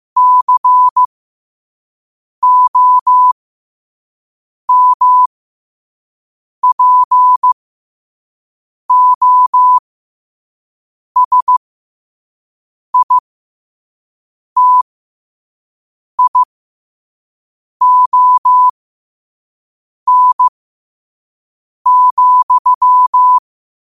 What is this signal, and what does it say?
Morse code, heard as a keyed tone, about 15 wpm, spaced as for 5 wpm, 1000 Hz. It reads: COMPOSITION,
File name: qotdfiles / 5wpm